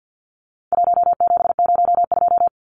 Morse code, heard as a keyed tone, about 50 wpm, 700 Hz. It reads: 1702
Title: Numbers - 4-Digit 50wpm